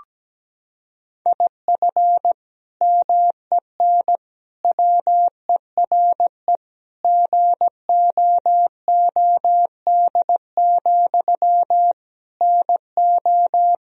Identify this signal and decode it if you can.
Morse code, heard as a keyed tone, about 17 wpm, 700 Hz. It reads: IF MEN WERE GOOD, NO
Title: Sentences from Top 1000 Words - Review Part 3 - 17wpm